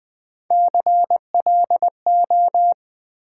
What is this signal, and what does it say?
Morse code, heard as a keyed tone, about 21 wpm, 700 Hz. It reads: CLO
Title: Build A Word - 1 Letter At A Time - 5-Letter Words - 20wpm